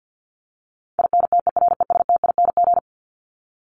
Morse code, heard as a keyed tone, about 50 wpm, 700 Hz. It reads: INTERESTING
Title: Top 751-1000 Words 50wpm